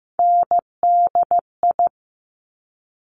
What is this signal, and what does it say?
Morse code, heard as a keyed tone, about 15 wpm, 700 Hz. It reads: NDI
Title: Top Three Letter Combinations in Words 15wpm